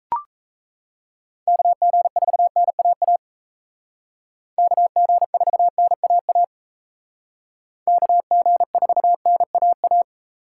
Morse code, heard as a keyed tone, about 35 wpm, 700 Hz. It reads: KG4NAA KG4NAA KG4NAA
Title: Speed Racing - USA 2x3 Call Signs 28wpm